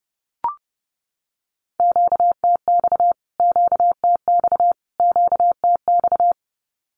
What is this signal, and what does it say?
Morse code, heard as a keyed tone, about 30 wpm, 700 Hz. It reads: QTX QTX QTX
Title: Q-Codes Repeated 3x 30wpm